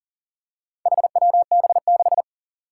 Morse code, heard as a keyed tone, about 42 wpm, 700 Hz. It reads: HWB6
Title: Four Challenging Letter-Number Combinations 40wpm